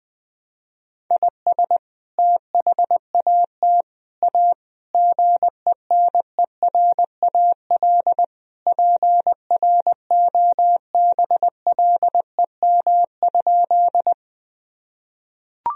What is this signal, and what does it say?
Morse code, heard as a keyed tone, about 20 wpm, 700 Hz. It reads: IS THAT A GENERAL PROBLEM?
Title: Sentences from Top 1000 Words - Review Part 3 - 20wpm